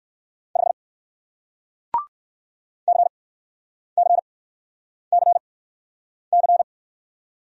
Morse code, heard as a keyed tone, about 54 wpm, spaced as for 9 wpm, 700 Hz. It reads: 5 CCCC
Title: Speed-Racing Single Letter-Number 45wpm